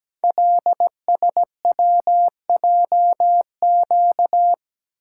Morse code, heard as a keyed tone, about 17 wpm, 700 Hz. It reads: LSWJQ
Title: Code Groups - 5 Letter:Numbers - 17wpm